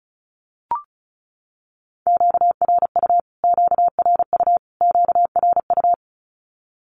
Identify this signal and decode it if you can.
Morse code, heard as a keyed tone, about 34 wpm, 700 Hz. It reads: QRU QRU QRU
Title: Q-Codes Repeated 3x 35wpm